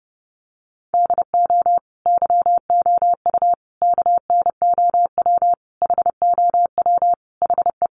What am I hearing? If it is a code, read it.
Morse code, heard as a keyed tone, about 30 wpm, 700 Hz. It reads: DO YOU KNOW HOW HE
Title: Sentences from Top 300 Words 30wpm